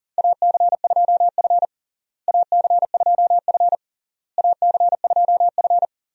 Morse code, heard as a keyed tone, about 37 wpm, 700 Hz. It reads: AC2F AC2F AC2F
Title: USA Call Signs - 4 Characters - Repeated 3x - 40wpm